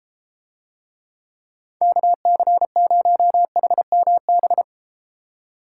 Morse code, heard as a keyed tone, about 34 wpm, 700 Hz. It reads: KC0HMB